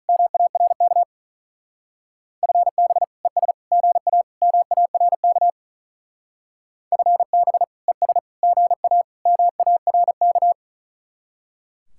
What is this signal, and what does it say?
Morse code, heard as a keyed tone, about 44 wpm, 700 Hz. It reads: MARK FB ES GA MARK FB ES GA MARK